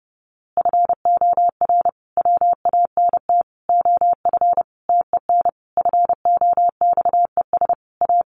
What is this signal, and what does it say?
Morse code, heard as a keyed tone, about 30 wpm, 700 Hz. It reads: FOR WANT OF TEN FOXES A
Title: A Sound Of Thunder 30wpm